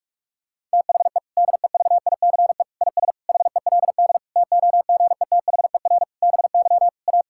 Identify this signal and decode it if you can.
Morse code, heard as a keyed tone, about 46 wpm, 700 Hz. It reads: THE DEVICE IS HELD TOGETHER BY A
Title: Sentences from Top 1600 Words 45wpm